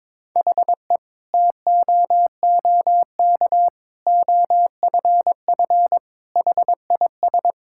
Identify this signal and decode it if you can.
Morse code, heard as a keyed tone, about 22 wpm, 700 Hz. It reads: HE TOOK OFF HIS
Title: Sentences from Top 1600 Words 22wpm